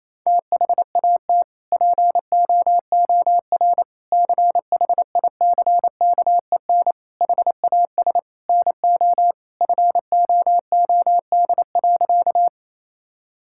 Morse code, heard as a keyed tone, about 28 wpm, 700 Hz. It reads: THAT POOR CHICKEN HAS NO FOOD.
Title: Sentences from Top 1000 Words - Review Part 5 - 28wpm